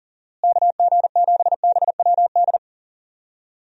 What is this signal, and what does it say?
Morse code, heard as a keyed tone, about 40 wpm, 700 Hz. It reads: KG7BWD
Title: USA Call Signs - 6 Characters - Repeated 3x - 40wpm